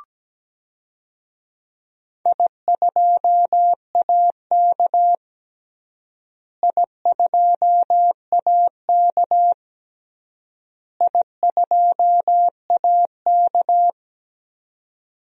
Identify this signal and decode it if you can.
Morse code, heard as a keyed tone, about 17 wpm, 700 Hz. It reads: I2AK I2AK I2AK
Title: DXCC-100 Call Signs - 4 Characters - Repeated 3x - 17wpm